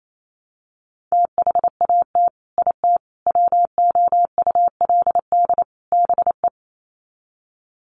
Morse code, heard as a keyed tone, about 28 wpm, 700 Hz. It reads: THAT IT WOULD BE
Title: Sets of 4 Words 28wpm